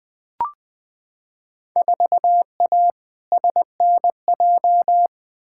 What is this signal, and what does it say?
Morse code, heard as a keyed tone, about 20 wpm, 700 Hz. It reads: 4A SNJ
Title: Field Day Exchanges Encore - 20wpm